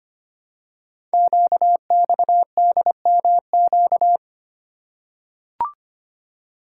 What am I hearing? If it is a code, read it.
Morse code, heard as a keyed tone, about 25 wpm, 700 Hz. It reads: QXDMQ
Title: Code Groups - 5 Letters - 25wpm